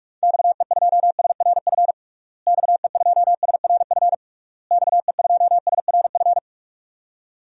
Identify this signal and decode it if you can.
Morse code, heard as a keyed tone, about 46 wpm, 700 Hz. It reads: XE2SRF XE2SRF XE2SRF
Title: DXCC-100 Call Signs - 6 Characters - Repeated 3x - 45wpm